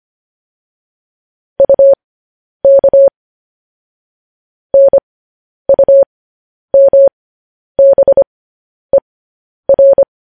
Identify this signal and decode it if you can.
Morse code, heard as a keyed tone, about 25 wpm, spaced as for 10 wpm, 550 Hz. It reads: UK NUMBER
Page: news headlines in morse code at 10 wpm